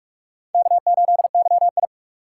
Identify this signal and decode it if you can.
Morse code, heard as a keyed tone, about 43 wpm, 700 Hz. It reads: K8YI
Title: USA Call Signs - 4 Characters - 2x Character Spacing - 45wpm